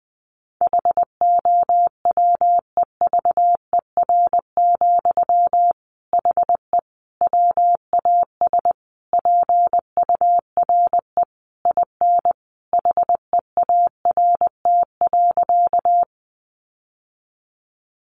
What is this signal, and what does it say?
Morse code, heard as a keyed tone, about 20 wpm, 700 Hz. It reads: HOWEVER, HE WAS PURE IN HEART.
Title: Sentences from Top 1500 Words 20wpm